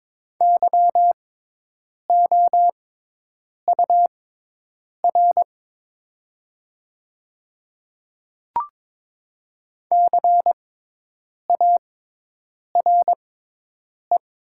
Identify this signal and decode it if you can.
Morse code, heard as a keyed tone, about 22 wpm, spaced as for 8 wpm, 700 Hz. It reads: YOUR CARE
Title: 6x Character Spacing - Top 300 Words - 22wpm